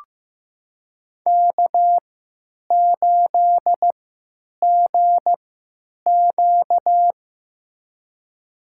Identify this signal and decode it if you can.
Morse code, heard as a keyed tone, about 15 wpm, spaced as for 9 wpm, 700 Hz. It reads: K8GQ
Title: USA Call Signs - 4 Characters - 3x Character Spacing - 15wpm